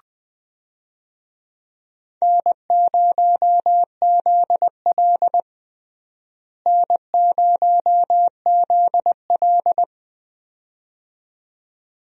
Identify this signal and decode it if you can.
Morse code, heard as a keyed tone, about 20 wpm, 700 Hz. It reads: N0ZL N0ZL
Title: USA Call Signs - 4 Characters - Repeated 2x - 20wpm